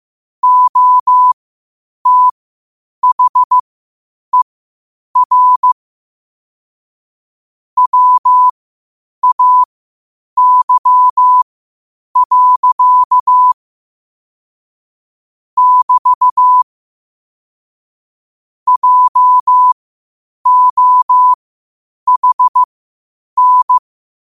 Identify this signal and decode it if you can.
Morse code, heard as a keyed tone, about 15 wpm, spaced as for 8 wpm, 1000 Hz. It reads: OTHER WAY. <BT> JOHN
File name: qotdfiles / 8wpm